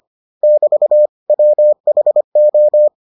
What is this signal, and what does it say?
Morse code, heard as a keyed tone, about 25 wpm, 600 Hz. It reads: <BT> WHO